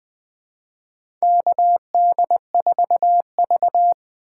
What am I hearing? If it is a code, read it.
Morse code, heard as a keyed tone, about 21 wpm, 700 Hz. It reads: KD4V
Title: USA Call Signs - 4 Characters - 3x Character Spacing - 20wpm